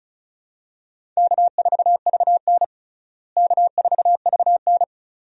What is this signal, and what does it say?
Morse code, heard as a keyed tone, about 36 wpm, 700 Hz. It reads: K4VN K4VN
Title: USA Call Signs - 4 Characters - Repeated 2x - 35wpm